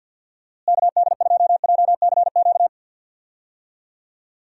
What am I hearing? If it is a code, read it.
Morse code, heard as a keyed tone, about 51 wpm, 700 Hz. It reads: KD2JCQ
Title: USA Call Signs - 6 Characters - Repeated 3x - 50wpm